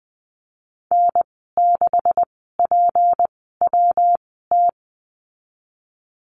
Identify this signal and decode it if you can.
Morse code, heard as a keyed tone, about 20 wpm, spaced as for 15 wpm, 700 Hz. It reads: N6PWT
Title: 173 Lesson 28g - Number 9 - Call Signs 20wpm